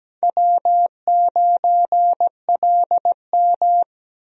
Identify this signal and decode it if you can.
Morse code, heard as a keyed tone, about 17 wpm, 700 Hz. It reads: W9LM
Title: USA Call Signs - 4 Characters - Repeated 2x - 17wpm